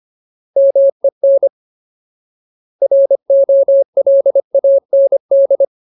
Morse code, heard as a keyed tone, about 25 wpm, 550 Hz. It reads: MEN ROLAND
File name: Ultimate Word List / 25WPM Ultimate